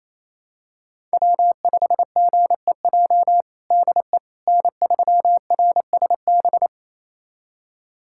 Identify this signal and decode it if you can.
Morse code, heard as a keyed tone, about 28 wpm, 700 Hz. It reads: W5GEJ DE N3RSB
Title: USA Callsign DE Callsign - 5 Characters - 28wpm